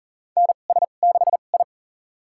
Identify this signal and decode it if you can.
Morse code, heard as a keyed tone, about 40 wpm, spaced as for 29 wpm, 700 Hz. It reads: NS6I
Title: USA Call Signs - 2x Character Spacing - 40wpm